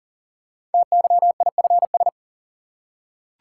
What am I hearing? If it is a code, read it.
Morse code, heard as a keyed tone, about 38 wpm, 700 Hz. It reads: TYIFS
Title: Code Groups - 5 Letter:Numbers - 40wpm